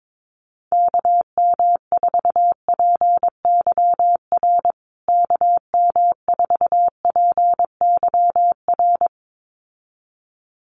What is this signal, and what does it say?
Morse code, heard as a keyed tone, about 22 wpm, 700 Hz. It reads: KM4PYR KM4PYR
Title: USA Call Signs Repeated 2x 22wpm